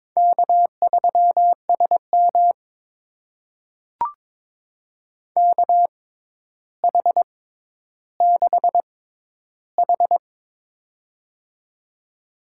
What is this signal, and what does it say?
Morse code, heard as a keyed tone, about 22 wpm, spaced as for 11 wpm, 700 Hz. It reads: K3SM KH6H
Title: USA Call Signs - 4 Characters - 6x Character Spacing - 22wpm